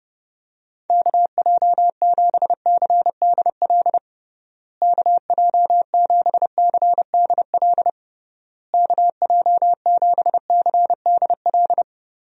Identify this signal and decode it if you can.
Morse code, heard as a keyed tone, about 30 wpm, 700 Hz. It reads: KJ7CDL KJ7CDL KJ7CDL
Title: USA Call Signs - 6 Characters - Repeated 3x - 30wpm